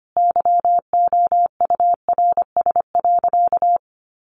Morse code, heard as a keyed tone, about 25 wpm, 700 Hz. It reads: YOURS.
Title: A Sound Of Thunder 25wpm